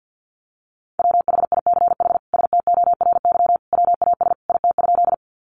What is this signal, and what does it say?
Morse code, heard as a keyed tone, about 50 wpm, 700 Hz. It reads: WHICH STORY WAS IT?
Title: Sentences from Top 300 Words 50wpm